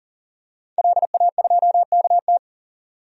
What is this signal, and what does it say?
Morse code, heard as a keyed tone, about 40 wpm, 700 Hz. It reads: LA2KT
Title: DXCC-100 Call Signs - 5 Characters - 40wpm